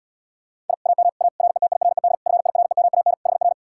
Morse code, heard as a keyed tone, about 79 wpm, 700 Hz. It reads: I AM A RETIRED FIREMAN HR